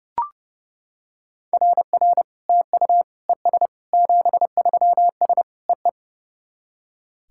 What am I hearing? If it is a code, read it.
Morse code, heard as a keyed tone, about 30 wpm, 700 Hz. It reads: RR TU ES 73S EE
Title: SOTA QSO Practice 30wpm